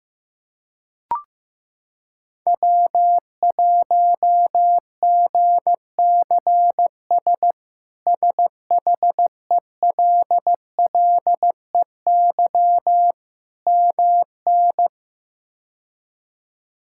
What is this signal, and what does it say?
Morse code, heard as a keyed tone, about 15 wpm, 700 Hz. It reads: W1GCS SHELLEY MN
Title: K1USN SST QSO Practice - 15wpm